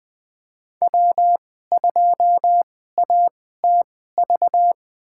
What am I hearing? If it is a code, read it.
Morse code, heard as a keyed tone, about 20 wpm, spaced as for 15 wpm, 700 Hz. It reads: W2ATV
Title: USA Call Signs - 5 Characters - 2x Character Spacing - 20wpm